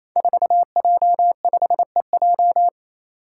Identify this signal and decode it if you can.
Morse code, heard as a keyed tone, about 28 wpm, 700 Hz. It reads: 4J5EJ
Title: Five Challenging Letter-Number Combinations 28wpm